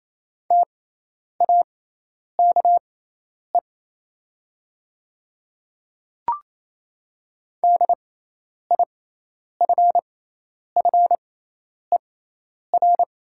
Morse code, heard as a keyed tone, about 28 wpm, spaced as for 10 wpm, 700 Hz. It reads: TAKE DIFFER
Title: 6x Character Spacing - Top 200 Words - 28wpm